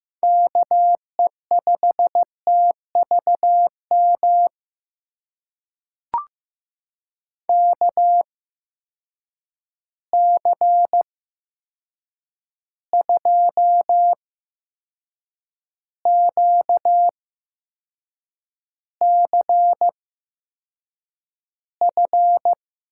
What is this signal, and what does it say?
Morse code, heard as a keyed tone, about 15 wpm, spaced as for 6 wpm, 700 Hz. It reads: KE5TVM KC2QCF
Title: USA Call Signs - 6 Characters - 8x Character Spacing - 15wpm